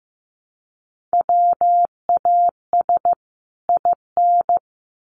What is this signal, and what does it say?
Morse code, heard as a keyed tone, about 15 wpm, 700 Hz. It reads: WAS IN
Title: Sets of 2 Words 15wpm